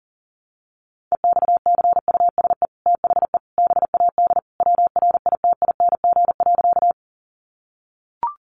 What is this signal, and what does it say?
Morse code, heard as a keyed tone, about 40 wpm, 700 Hz. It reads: EXCUSE THE BAD WRITING.
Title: Sentences from Top 800 Words 40wpm